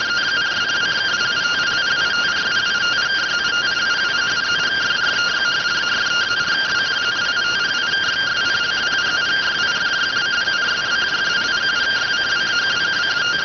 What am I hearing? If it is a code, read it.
Two-tone teleprinter signal, about 170 Hz shift.